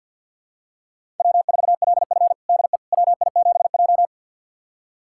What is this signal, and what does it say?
Morse code, heard as a keyed tone, about 50 wpm, 700 Hz. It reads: W4LF DE WI7J
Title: USA Callsign DE Callsign - 4 Characters - 50wpm